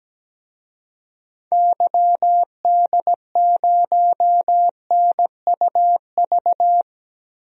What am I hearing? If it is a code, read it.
Morse code, heard as a keyed tone, about 17 wpm, 700 Hz. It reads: YD0NUV
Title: DXCC-100 Call Signs - 6 Characters - 17wpm